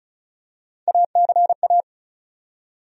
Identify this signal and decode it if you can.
Morse code, heard as a keyed tone, about 34 wpm, 700 Hz. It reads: ACA
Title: USA Call Sign Postfixes - 3 Letters 35wpm